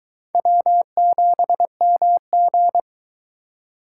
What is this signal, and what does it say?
Morse code, heard as a keyed tone, about 22 wpm, 700 Hz. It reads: W7MG